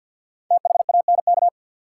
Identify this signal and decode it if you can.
Morse code, heard as a keyed tone, about 52 wpm, 700 Hz. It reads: THANK
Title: Sets of 2 Words - New Word Drill - 50wpm